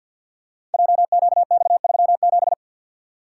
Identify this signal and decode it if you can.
Morse code, heard as a keyed tone, about 50 wpm, 690 Hz. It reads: JQX37
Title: Code Groups - 5 Letter:Numbers - 50wpm